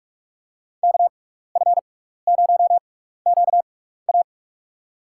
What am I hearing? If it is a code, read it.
Morse code, heard as a keyed tone, about 46 wpm, spaced as for 15 wpm, 700 Hz. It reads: KF0QA
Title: USA Call Signs - 5 Characters - 6x Character Spacing - 45wpm